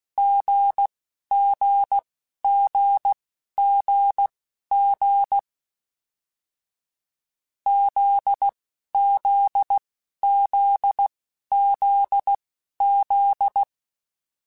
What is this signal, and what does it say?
Morse code, heard as a keyed tone, about 16 wpm, spaced as for 11 wpm, 800 Hz. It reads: GGGGG ZZZZZ
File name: Short Runs / Short Runs-mp3s / GZ GZ7 @10